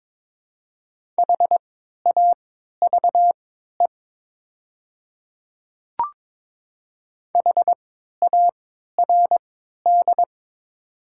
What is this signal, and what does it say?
Morse code, heard as a keyed tone, about 22 wpm, spaced as for 13 wpm, 700 Hz. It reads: HAVE HARD
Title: 3x Character Spacing - Top 500 Words - 22wpm